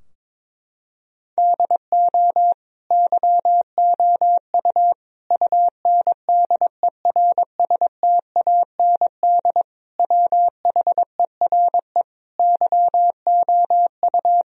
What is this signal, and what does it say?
Morse code, heard as a keyed tone, about 22 wpm, 700 Hz. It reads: DO YOU UNDERSTAND WHERE YOU
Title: Sentences from Top 1000 Words - Review Part 4 - 22wpm